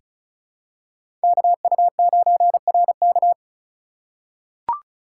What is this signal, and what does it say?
Morse code, heard as a keyed tone, about 34 wpm, 700 Hz. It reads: KU9RK
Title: K1USN SST QSO Practice - 35wpm